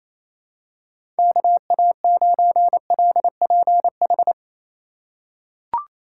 Morse code, heard as a keyed tone, about 28 wpm, 700 Hz. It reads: KA9LPH
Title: USA Call Signs - 6 Characters - 2x Character Spacing - 28wpm